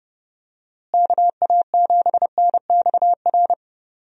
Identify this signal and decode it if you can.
Morse code, heard as a keyed tone, about 32 wpm, 700 Hz. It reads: KA7NXR
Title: USA Call Signs - 6 Characters - Repeated 2x - 30wpm